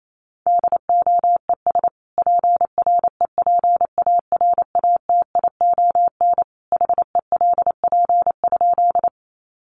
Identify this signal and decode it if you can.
Morse code, heard as a keyed tone, about 28 wpm, 700 Hz. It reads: DOES PREPARATION HELP?
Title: Sentences from Top 1400 Words 28wpm